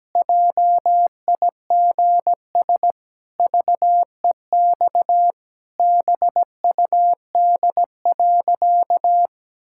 Morse code, heard as a keyed tone, about 17 wpm, 700 Hz. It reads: JIGS VEX BUD.
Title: Pangram Challenge v2 - 17wpm